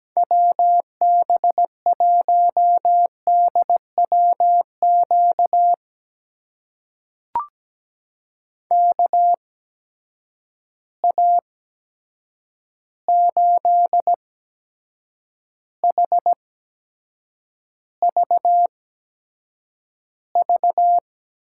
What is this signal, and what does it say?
Morse code, heard as a keyed tone, about 17 wpm, spaced as for 7 wpm, 700 Hz. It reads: WB1DWQ KA8HVV
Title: USA Call Signs - 6 Characters - 8x Character Spacing - 17wpm